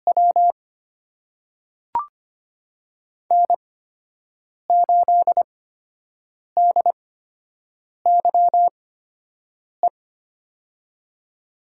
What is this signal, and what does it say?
Morse code, heard as a keyed tone, about 25 wpm, spaced as for 7 wpm, 700 Hz. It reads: W N8DYE
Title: USA Call Signs - 5 Characters - 8x Character Spacing - 25wpm